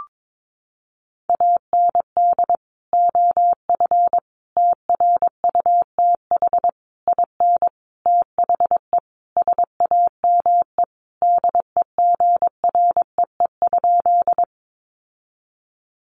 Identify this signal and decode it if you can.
Morse code, heard as a keyed tone, about 22 wpm, 700 Hz. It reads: AND OF TRUTH IN THE SAME DEGREE?
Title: Sentences from Top 1500 Words 22wpm